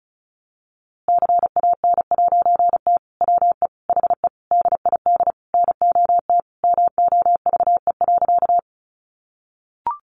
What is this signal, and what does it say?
Morse code, heard as a keyed tone, about 34 wpm, 700 Hz. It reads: CAN'T WE HE DID NOT MOVE.
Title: A Sound Of Thunder 35wpm